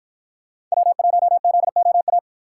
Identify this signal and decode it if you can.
Morse code, heard as a keyed tone, about 50 wpm, 700 Hz. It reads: W1ZOA